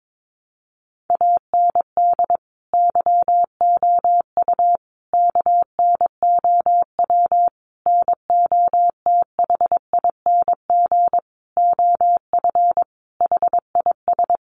Morse code, heard as a keyed tone, about 22 wpm, 700 Hz. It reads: AND YOU KNOW NOTHING OF HIS
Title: Sentences from Top 1200 Words 22wpm